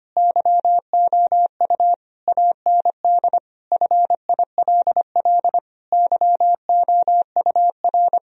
Morse code, heard as a keyed tone, about 25 wpm, 700 Hz. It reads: YOU AND FILL YOUR
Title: Sentences from Top 1000 Words - Review Part 3 - 25wpm